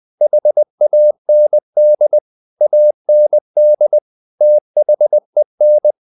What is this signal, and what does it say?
Morse code, heard as a keyed tone, about 20 wpm, 600 Hz. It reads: HAND AND THEN
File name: the old man and the sea part 2 20wpm